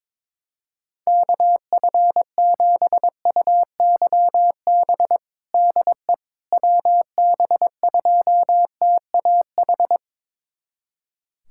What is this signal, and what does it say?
Morse code, heard as a keyed tone, about 22 wpm, 700 Hz. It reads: KF7UYB DE WB2TAH
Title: USA Callsign DE Callsign - 6 Characters - 22wpm